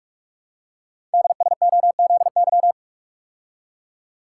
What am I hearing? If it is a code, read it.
Morse code, heard as a keyed tone, about 44 wpm, 700 Hz. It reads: DSOZY
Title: Code Groups - 5 Letters - 45wpm